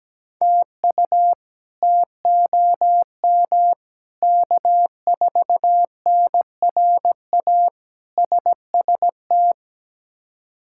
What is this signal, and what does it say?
Morse code, heard as a keyed tone, about 17 wpm, 700 Hz. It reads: TU TOM K4NRA SST
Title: K1USN SST QSO Practice - 17wpm